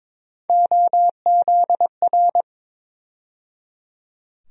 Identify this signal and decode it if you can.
Morse code, heard as a keyed tone, about 22 wpm, 700 Hz. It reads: OZR
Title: USA Call Sign Postfixes - 3 Letters 22wpm